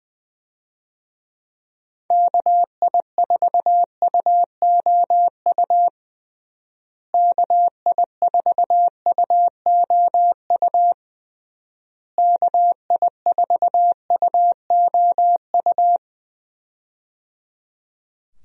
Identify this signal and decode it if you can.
Morse code, heard as a keyed tone, about 21 wpm, 700 Hz. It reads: KI4UOU KI4UOU KI4UOU
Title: USA Call Signs - 6 Characters - Repeated 3x - 20wpm